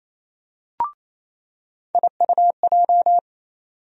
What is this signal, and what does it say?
Morse code, heard as a keyed tone, about 27 wpm, 700 Hz. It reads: IUJ
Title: USA Call Sign Postfixes - 3 Letters 28wpm